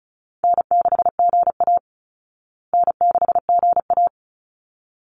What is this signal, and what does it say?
Morse code, heard as a keyed tone, about 34 wpm, 700 Hz. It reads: N6GA N6GA
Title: Easy Call Signs Repeated 2x - 35wpm